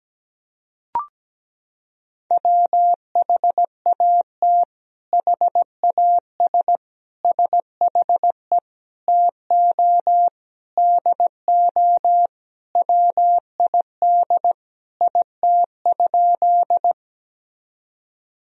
Morse code, heard as a keyed tone, about 17 wpm, 700 Hz. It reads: WHAT HAS SHE TO DO WID IT?
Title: Sentences from Top 100 Words and Abbreviations 17wpm